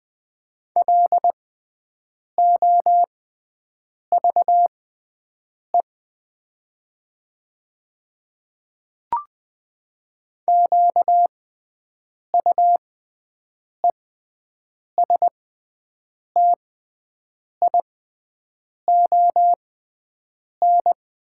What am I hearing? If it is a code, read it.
Morse code, heard as a keyed tone, about 20 wpm, spaced as for 7 wpm, 700 Hz. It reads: LOVE QUESTION
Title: 6x Character Spacing - Top 500 Words - 20wpm